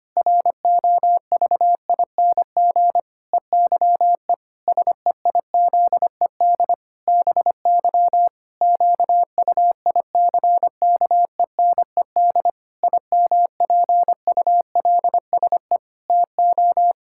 Morse code, heard as a keyed tone, about 25 wpm, 700 Hz. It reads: ROVING EYE SEIZED BY QUICKENED IMPULSE TO
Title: Pangram Challenge v2 - 25wpm